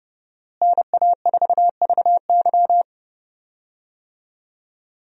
Morse code, heard as a keyed tone, about 31 wpm, 700 Hz. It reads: NA4VY
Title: USA Call Signs - 5 Characters - Repeated 2x - 30wpm